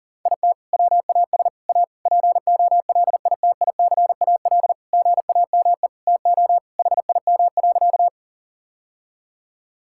Morse code, heard as a keyed tone, about 40 wpm, 700 Hz. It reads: IT WAS A POLITICAL GAME TO HIM.
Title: Sentences from Top 1000 Words - Review Part 5 - 40wpm